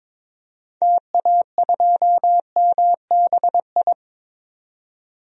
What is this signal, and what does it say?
Morse code, heard as a keyed tone, about 22 wpm, 700 Hz. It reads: TA2MBI
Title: DXCC-100 Call Signs - 6 Characters - 22wpm